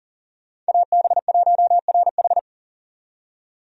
Morse code, heard as a keyed tone, about 40 wpm, 700 Hz. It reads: AB1RH
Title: USA Call Signs - 5 Characters - 40wpm